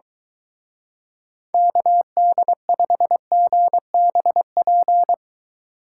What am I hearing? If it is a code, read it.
Morse code, heard as a keyed tone, about 23 wpm, 700 Hz. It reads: KD5GBP